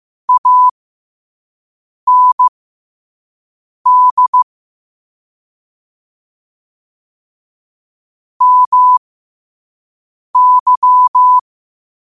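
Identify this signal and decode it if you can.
Morse code, heard as a keyed tone, about 15 wpm, spaced as for 5 wpm, 1000 Hz. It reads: AND MY
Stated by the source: Morse code practice at 5 wpm.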